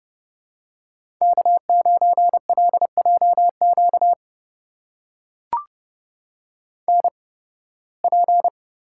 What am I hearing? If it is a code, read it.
Morse code, heard as a keyed tone, about 30 wpm, spaced as for 20 wpm, 700 Hz. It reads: K9LJQ NP
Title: USA Call Signs - 5 Characters - 8x Character Spacing - 30wpm